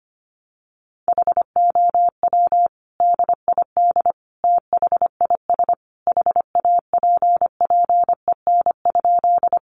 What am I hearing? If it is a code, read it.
Morse code, heard as a keyed tone, about 25 wpm, 700 Hz. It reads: HOW DID THIS HAPPEN?
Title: Sentences from Top 400 Words 25wpm